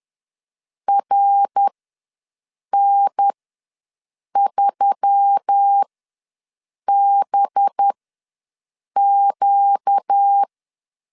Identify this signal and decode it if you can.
Morse code, heard as a keyed tone, about 10 wpm, spaced as for 6 wpm, 800 Hz. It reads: RN3BQ
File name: LEZIONI / 5 LBVC / 5 LB ws 10wpm 3616